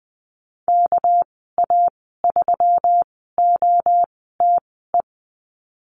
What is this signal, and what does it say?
Morse code, heard as a keyed tone, about 20 wpm, spaced as for 14 wpm, 700 Hz. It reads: KA3OTE
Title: 213 Lesson 34g - Number 8 - Call Signs 20wpm